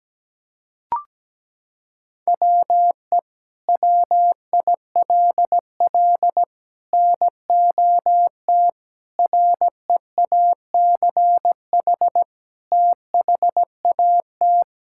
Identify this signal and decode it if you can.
Morse code, heard as a keyed tone, about 17 wpm, 700 Hz. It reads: WE WILL NOT REACH THAT
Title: Sentences from Top 1000 Words - Review Part 5 - 17wpm